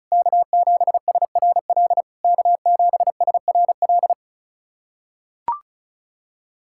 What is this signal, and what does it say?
Morse code, heard as a keyed tone, about 36 wpm, 690 Hz. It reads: K7SRL K7SRL
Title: SOTA QSO Practice 35wpm